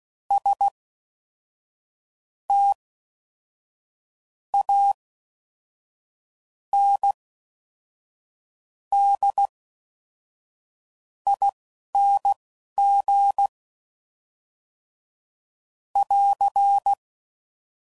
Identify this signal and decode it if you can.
Morse code, heard as a keyed tone, about 16 wpm, spaced as for 5 wpm, 800 Hz. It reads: STANDING <AR>